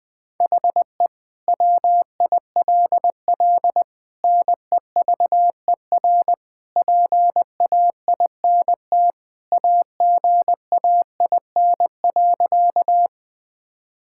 Morse code, heard as a keyed tone, about 20 wpm, 700 Hz. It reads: HE WILL NEVER PAINT AGAIN.
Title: Sentences from Top 1000 Words - Review Part 6 - 20wpm